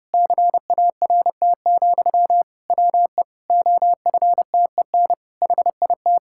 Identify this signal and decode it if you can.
Morse code, heard as a keyed tone, about 32 wpm, 700 Hz. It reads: CART, WE OFTEN HIT
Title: Pangram Challenge v2 - 30wpm